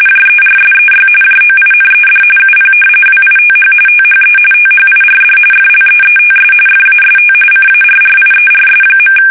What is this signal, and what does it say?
RTTY, about 45.45 baud. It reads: JERSEY JOINED UP WITH THE THIRD FLEETS FAST CARRIERS.
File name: fsk850 / battleshipnj